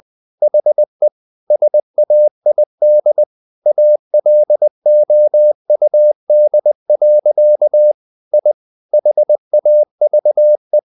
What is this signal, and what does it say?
Morse code, heard as a keyed tone, about 20 wpm, 600 Hz. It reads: HE SAID ALOUD. I HAVE
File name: the old man and the sea part 2 20wpm